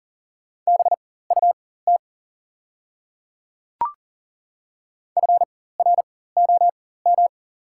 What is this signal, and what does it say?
Morse code, heard as a keyed tone, about 39 wpm, spaced as for 19 wpm, 700 Hz. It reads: BUT FROM
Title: 4x Character Spacing - Top 100 Words - 40wpm